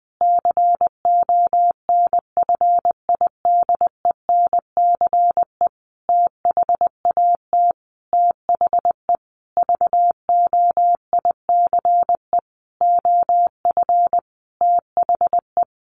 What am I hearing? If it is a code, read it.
Morse code, heard as a keyed tone, about 20 wpm, 700 Hz. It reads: CONFIDENCE THAT THE VOICE OF THE
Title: Sentences from Top 1500 Words 20wpm